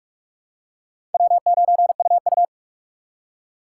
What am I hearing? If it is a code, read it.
Morse code, heard as a keyed tone, about 46 wpm, 700 Hz. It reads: W9UU
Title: USA Call Signs - 4 Characters - 4x Character Spacing - 45wpm